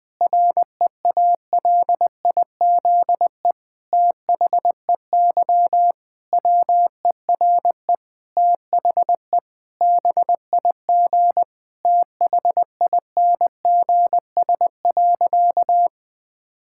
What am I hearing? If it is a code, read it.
Morse code, heard as a keyed tone, about 20 wpm, 700 Hz. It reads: REALIZE THEY WERE THE BIG THINGS.
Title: Difficult Sentences with Hints - 20wpm